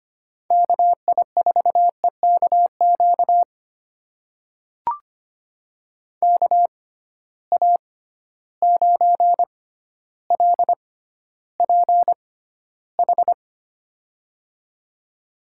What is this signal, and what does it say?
Morse code, heard as a keyed tone, about 25 wpm, spaced as for 13 wpm, 700 Hz. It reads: KI4EKQ KA9LPH
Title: USA Call Signs - 6 Characters - 6x Character Spacing - 25wpm